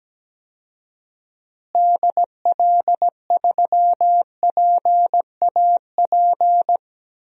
Morse code, heard as a keyed tone, about 17 wpm, 700 Hz. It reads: DL3PAP
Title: DXCC-100 Call Signs - 6 Characters - Repeated 2x - 17wpm